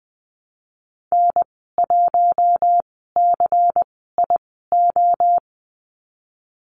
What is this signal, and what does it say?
Morse code, heard as a keyed tone, about 20 wpm, spaced as for 15 wpm, 700 Hz. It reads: N1CIO
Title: 128 Lesson 21g - Number 6 - Call Signs 20wpm